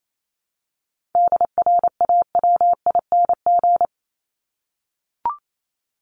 Morse code, heard as a keyed tone, about 28 wpm, 700 Hz. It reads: DRAWING
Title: ING Suffix - Top 100 Words with Base Words 28wpm